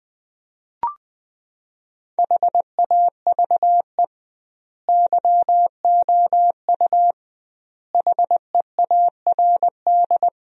Morse code, heard as a keyed tone, about 20 wpm, 700 Hz. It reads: HAVE YOU HEARD
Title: 2x Word-Spacing - Sentences from Top 400 Words - 20wpm